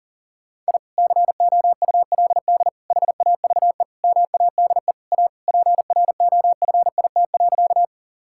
Morse code, heard as a keyed tone, about 40 wpm, 700 Hz. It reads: I COULD HAVE MADE A PROFIT.
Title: Sentences from Top 1000 Words - Review Part 5 - 40wpm